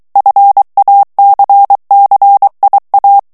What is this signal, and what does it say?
Morse code, heard as a keyed tone, about 23 wpm, 800 Hz. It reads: FACCIA
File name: morse mp3 / ITALIEN texte clair 1 vitesse 22 mots minute